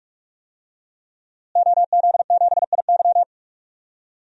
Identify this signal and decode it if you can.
Morse code, heard as a keyed tone, about 46 wpm, 700 Hz. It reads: OZ7IY
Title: DXCC-100 Call Signs - 5 Characters - Repeated 3x - 45wpm